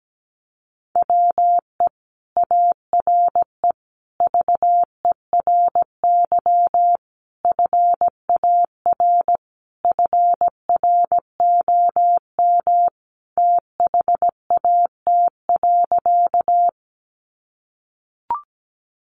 Morse code, heard as a keyed tone, about 17 wpm, 700 Hz. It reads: WE ARE VERY FAR FROM THAT.
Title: Sentences from Top 300 Words 17wpm